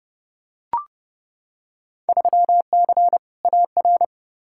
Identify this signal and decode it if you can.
Morse code, heard as a keyed tone, about 29 wpm, 700 Hz. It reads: 3C AR
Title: Field Day Exchanges Encore - 30wpm